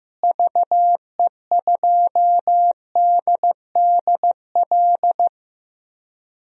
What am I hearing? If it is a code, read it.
Morse code, heard as a keyed tone, about 15 wpm, 700 Hz. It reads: VE2DDL
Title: DXCC-100 Call Signs - 6 Characters - Repeated 2x - 15wpm